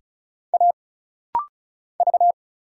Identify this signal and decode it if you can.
Morse code, heard as a keyed tone, about 36 wpm, 700 Hz. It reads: A V
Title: Single Letter-Number Alt - Rapid-Fire 35wpm